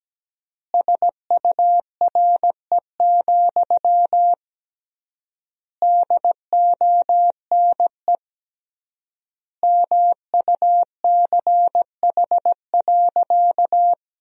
Morse code, heard as a keyed tone, about 17 wpm, 700 Hz. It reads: SURE, DONE MUCH.
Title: 3x Word-Spacing - Sentences from Top 500 Words - 17wpm